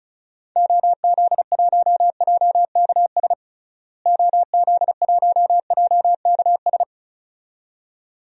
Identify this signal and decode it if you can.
Morse code, heard as a keyed tone, about 36 wpm, 700 Hz. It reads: OZ1JKS OZ1JKS
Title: DXCC-100 Call Signs - 6 Characters - Repeated 2x - 35wpm